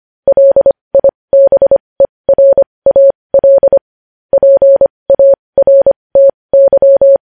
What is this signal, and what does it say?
Morse code, heard as a keyed tone, about 25 wpm, spaced as for 20 wpm, 550 Hz. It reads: LIBERAL PARTY
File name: podcast / News headlines-1x-25x20